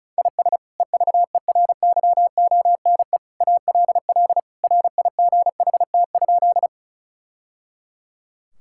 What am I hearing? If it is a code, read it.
Morse code, heard as a keyed tone, about 36 wpm, 700 Hz. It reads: IS EVERYONE ALL RIGHT?
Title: Sentences from Top 1000 Words - Review Part 5 - 35wpm